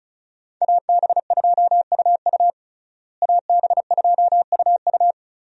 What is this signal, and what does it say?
Morse code, heard as a keyed tone, about 36 wpm, 700 Hz. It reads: AB2UU AB2UU
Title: USA Call Signs - 5 Characters - Repeated 3x - 35wpm